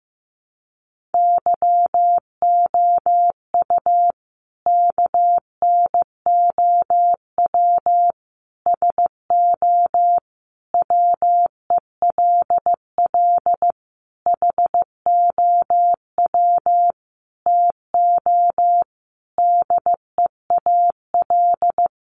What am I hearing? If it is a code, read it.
Morse code, heard as a keyed tone, about 15 wpm, 700 Hz. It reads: YOU KNOW SO WELL HOW TO DEAL
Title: Sentences from Top 800 Words 15wpm